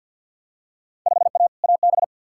Morse code, heard as a keyed tone, about 48 wpm, 700 Hz. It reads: 5A AB
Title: Field Day Exchanges Encore - 50wpm